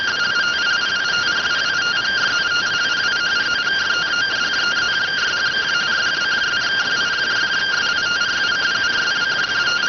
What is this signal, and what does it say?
Two-tone teleprinter signal, about 170 Hz shift.